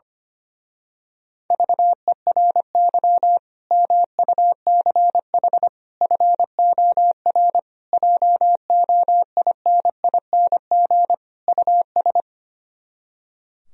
Morse code, heard as a keyed tone, about 24 wpm, 700 Hz. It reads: VERY MUCH FOR JOINING US
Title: Speed-Racing - Sets of 5 Words 25wpm